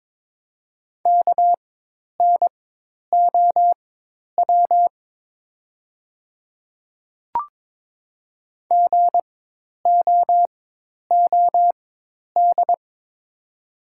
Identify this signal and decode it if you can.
Morse code, heard as a keyed tone, about 22 wpm, spaced as for 10 wpm, 700 Hz. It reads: KNOW GOOD
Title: 4x Character Spacing - Top 200 Words - 22wpm